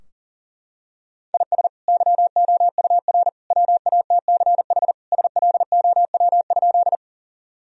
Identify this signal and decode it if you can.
Morse code, heard as a keyed tone, about 37 wpm, 700 Hz. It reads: IS YOUR WATCH SLOW?
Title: Sentences from Top 1000 Words - Review Part 2 - 40wpm